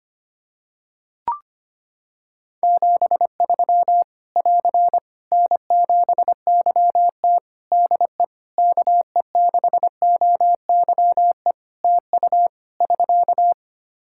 Morse code, heard as a keyed tone, about 25 wpm, 700 Hz. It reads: 73 <AR> N7YT DE KE6OYE TU <SK>
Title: Practice QSO - Exchange 4 - Ending - 25wpm